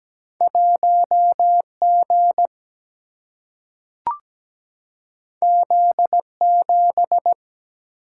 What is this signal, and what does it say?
Morse code, heard as a keyed tone, about 17 wpm, 700 Hz. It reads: 1G Z7
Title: Single-Letter Single-Number 17wpm